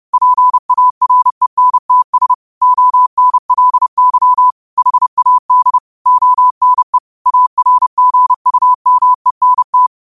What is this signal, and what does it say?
Morse code, heard as a keyed tone, about 30 wpm, 1000 Hz. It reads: PARENTS ONLY HAD ONE ARGUMENT
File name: qotdfiles / 30wpm